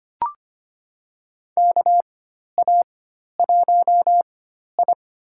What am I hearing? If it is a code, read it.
Morse code, heard as a keyed tone, about 25 wpm, spaced as for 12 wpm, 700 Hz. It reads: KA1I
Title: USA Call Signs - 4x Character Spacing - 25wpm